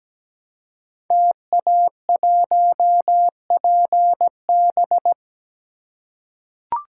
Morse code, heard as a keyed tone, about 17 wpm, 700 Hz. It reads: TA1PB
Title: Code Groups - 5 Letter:Numbers - 17wpm